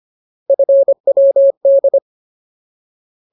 Morse code, heard as a keyed tone, about 25 wpm, 550 Hz. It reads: FWD
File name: Ultimate Word List / 25WPM Ultimate